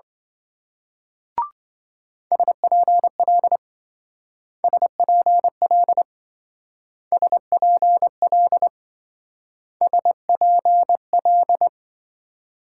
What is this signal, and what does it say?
Morse code, heard as a keyed tone, about 25 wpm, 700 Hz. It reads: SPL SPL SPL SPL
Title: Speed-Racing Multi-letter Phonemes - 20wpm